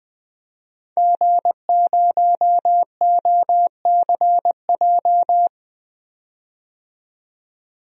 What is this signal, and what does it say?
Morse code, heard as a keyed tone, about 20 wpm, 700 Hz. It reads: G0OCJ
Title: DXCC-100 Call Signs - 5 Characters - 20wpm